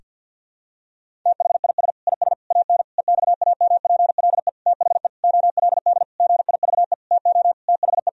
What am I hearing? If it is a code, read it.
Morse code, heard as a keyed tone, about 50 wpm, 700 Hz. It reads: THIS IS AN EXAMPLE THE OLD GIVE TO THE
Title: Sentences from Top 1000 Words - Review Part 1 - 50wpm